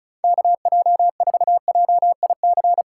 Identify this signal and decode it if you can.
Morse code, heard as a keyed tone, about 36 wpm, 700 Hz. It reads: KJ4JIC
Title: USA Call Signs - 4x Character Spacing - 35wpm